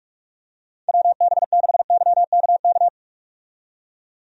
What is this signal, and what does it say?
Morse code, heard as a keyed tone, about 46 wpm, 700 Hz. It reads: WB6YKK
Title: USA Call Signs - 6 Characters - 45wpm